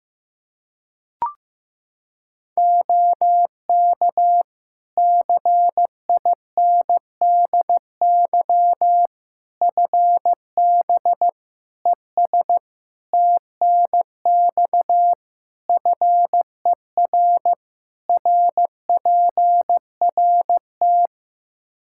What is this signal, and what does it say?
Morse code, heard as a keyed tone, about 15 wpm, 700 Hz. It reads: OK CINDY FB ES TNX FER RPRT
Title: K1USN SST QSO Practice 15wpm